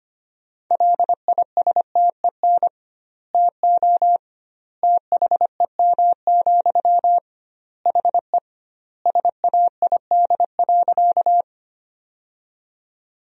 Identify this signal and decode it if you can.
Morse code, heard as a keyed tone, about 25 wpm, 700 Hz. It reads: LISTEN TO THEM, HE SAID.
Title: 2x Word-Spacing - Sentences from Top 400 Words - 25wpm